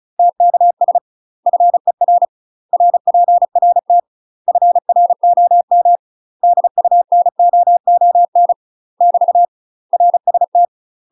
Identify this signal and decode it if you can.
Morse code, heard as a keyed tone, about 36 wpm, 700 Hz. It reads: TKS FER RPRT FROM DUNOON <BT> RST
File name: UK MASTER QSOs TRAINING / 35WPM / UK MASTER QSOs 0041